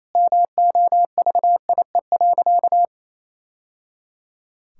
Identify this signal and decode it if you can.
Morse code, heard as a keyed tone, about 28 wpm, 700 Hz. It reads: MOVIE.
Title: Sentences from Top 1000 Words - Review Part 5 - 28wpm